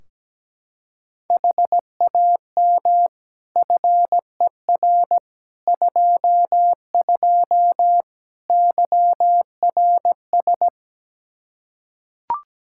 Morse code, heard as a keyed tone, about 17 wpm, 700 Hz. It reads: HAM FER 22 YRS
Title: Practice QSO - Exchange 3 - More Info - 17wpm